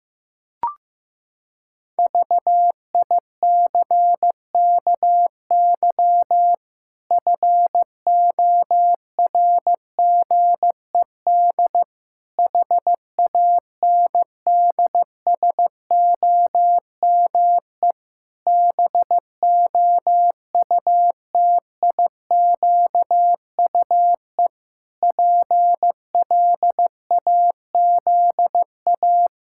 Morse code, heard as a keyed tone, about 15 wpm, 700 Hz. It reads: VICKY FORGED HANDSOME BOUTIQUE PLAZA
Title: Pangram Challenge v2 - 15wpm